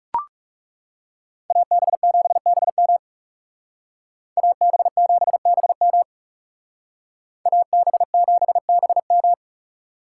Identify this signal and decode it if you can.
Morse code, heard as a keyed tone, about 38 wpm, 700 Hz. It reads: AB7BM AB7BM AB7BM